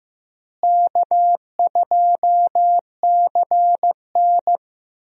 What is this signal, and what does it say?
Morse code, heard as a keyed tone, about 15 wpm, 700 Hz. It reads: K2CN
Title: USA Call Signs - 4 Characters - Repeated 2x - 15wpm